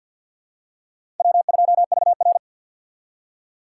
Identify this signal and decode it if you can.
Morse code, heard as a keyed tone, about 52 wpm, 690 Hz. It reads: W2VR
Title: USA Call Signs - 4 Characters - 6x Character Spacing - 50wpm